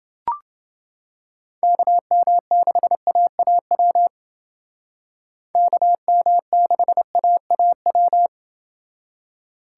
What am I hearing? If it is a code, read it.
Morse code, heard as a keyed tone, about 29 wpm, 700 Hz. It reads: KM6AAW KM6AAW